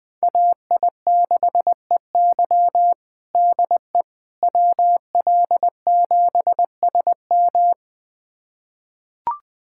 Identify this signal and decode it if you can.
Morse code, heard as a keyed tone, about 20 wpm, 700 Hz. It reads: AI6EY DE WL7SM
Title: USA Callsign DE Callsign - 5 Characters - 20wpm